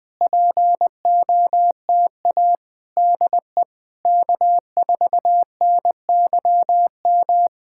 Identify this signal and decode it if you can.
Morse code, heard as a keyed tone, about 20 wpm, 700 Hz. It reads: POTA DE K4NYM
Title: POTA QSO Practice - 20wpm